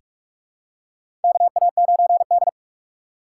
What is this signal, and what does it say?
Morse code, heard as a keyed tone, about 44 wpm, 690 Hz. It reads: KA9D
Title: USA Call Signs - 4 Characters - 2x Character Spacing - 45wpm